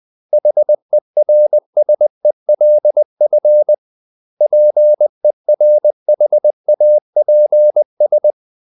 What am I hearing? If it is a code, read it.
Morse code, heard as a keyed tone, about 20 wpm, 600 Hz. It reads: HERSELF PERHAPS
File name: CW / Practice / 500MostCommonEnglishWords20